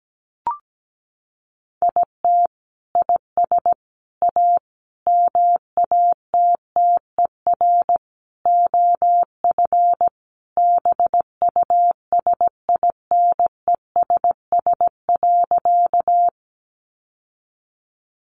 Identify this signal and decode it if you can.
Morse code, heard as a keyed tone, about 17 wpm, 700 Hz. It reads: IT IS A MATTER OF BUSINESS.
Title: Sentences from Top 700 Words 17wpm